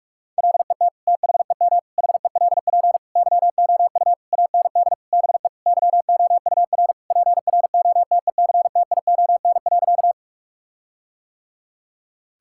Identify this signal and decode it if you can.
Morse code, heard as a keyed tone, about 41 wpm, 700 Hz. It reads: LET THEM HELP YOU AND BE YOUR PROTECTION.
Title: Sentences from Top 1600 Words 45wpm